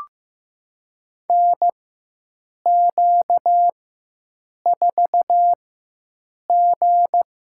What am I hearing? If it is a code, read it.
Morse code, heard as a keyed tone, about 15 wpm, spaced as for 7 wpm, 700 Hz. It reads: NQ4G
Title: USA Call Signs - 4x Character Spacing - 15wpm